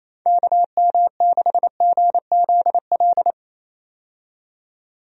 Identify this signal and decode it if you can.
Morse code, heard as a keyed tone, about 27 wpm, 700 Hz. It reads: KM6GZL
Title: USA Call Signs - 6 Characters - 28wpm